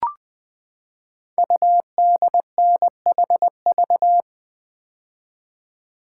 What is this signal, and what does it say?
Morse code, heard as a keyed tone, about 20 wpm, 700 Hz. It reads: UDNHV
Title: Code Groups - 5 Letters - 20wpm